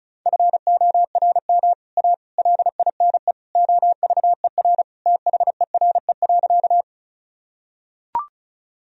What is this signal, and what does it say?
Morse code, heard as a keyed tone, about 36 wpm, 700 Hz. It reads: FORM A LINE OVER THERE.
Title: Sentences from Top 1000 Words - Review Part 1 - 35wpm